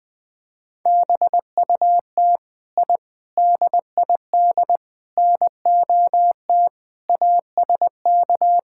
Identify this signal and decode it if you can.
Morse code, heard as a keyed tone, about 20 wpm, 700 Hz. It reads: BUT I DID NOT ASK
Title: Sentences from Top 1000 Words - Review Part 5 - 20wpm